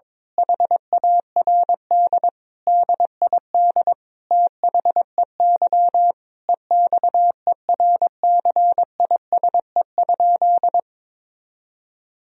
Morse code, heard as a keyed tone, about 22 wpm, 700 Hz. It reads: HARD DID THEY EXERCISE?
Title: Sentences from Top 1000 Words - Review Part 4 - 22wpm